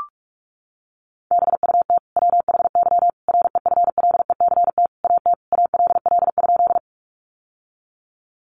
Morse code, heard as a keyed tone, about 44 wpm, 700 Hz. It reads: BUT WHY REFLECT AT ALL?
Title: Sentences from Top 1500 Words 45wpm